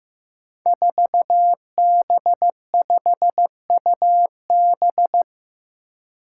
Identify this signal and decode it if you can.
Morse code, heard as a keyed tone, about 15 wpm, 700 Hz. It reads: 4B5UB
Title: Five Challenging Letter-Number Combinations 15wpm